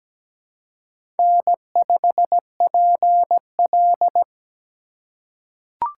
Morse code, heard as a keyed tone, about 17 wpm, 700 Hz. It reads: N5PL
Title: USA Call Signs - 4 Characters - 4x Character Spacing - 17wpm